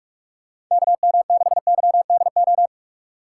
Morse code, heard as a keyed tone, about 44 wpm, 700 Hz. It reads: KM6YDO
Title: USA Call Signs - 6x Character Spacing - 45wpm